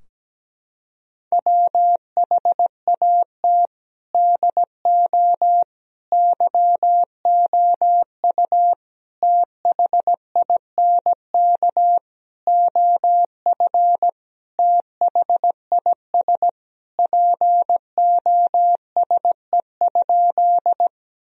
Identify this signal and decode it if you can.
Morse code, heard as a keyed tone, about 17 wpm, 700 Hz. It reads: WHAT DO YOU THINK OF THIS POSE?
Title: Sentences from Top 1000 Words - Review Part 2 - 17wpm